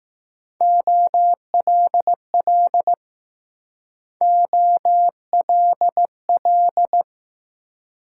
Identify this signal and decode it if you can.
Morse code, heard as a keyed tone, about 20 wpm, 700 Hz. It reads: OLL OLL